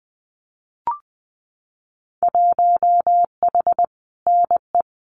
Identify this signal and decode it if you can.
Morse code, heard as a keyed tone, about 20 wpm, 700 Hz. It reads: 1H NE
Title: Winter Field Day Exchanges Encore 20wpm